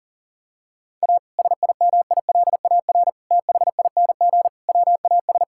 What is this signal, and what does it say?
Morse code, heard as a keyed tone, about 40 wpm, 700 Hz. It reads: A SIMILAR THING WAS
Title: Sentences from Top 1000 Words - Review Part 5 - 40wpm